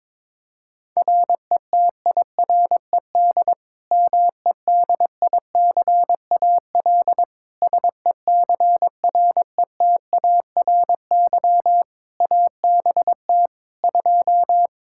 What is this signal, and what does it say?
Morse code, heard as a keyed tone, about 22 wpm, 700 Hz. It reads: RETIRED MEDICAL SECRETARY ABT 2
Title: Practice QSO - Exchange 3 - More Info - 22wpm